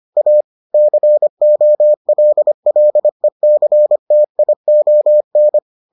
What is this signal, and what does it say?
Morse code, heard as a keyed tone, about 25 wpm, 600 Hz. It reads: A COLLECTION